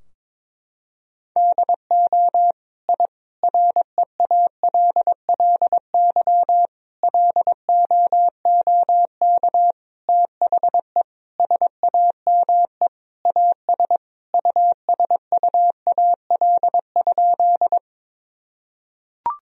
Morse code, heard as a keyed tone, about 22 wpm, 700 Hz. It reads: DO I REALLY LOOK THE SAME AS USUAL?
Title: Sentences from Top 1000 Words - Review Part 2 - 22wpm